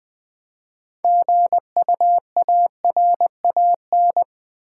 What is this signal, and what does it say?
Morse code, heard as a keyed tone, about 20 wpm, 700 Hz. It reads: GUARAN
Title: Build A Word - 3 Letters At A Time - 9-Letter Words - 20wpm